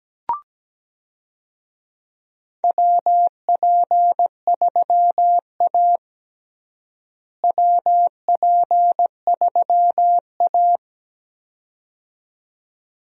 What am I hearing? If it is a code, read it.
Morse code, heard as a keyed tone, about 17 wpm, 700 Hz. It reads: WP3A WP3A
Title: USA Call Signs - 4 Characters - Repeated 2x - 17wpm